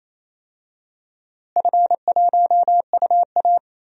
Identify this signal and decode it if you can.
Morse code, heard as a keyed tone, about 28 wpm, 700 Hz. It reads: F1UA
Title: DXCC-100 Call Signs - 4 Characters - Repeated 2x - 28wpm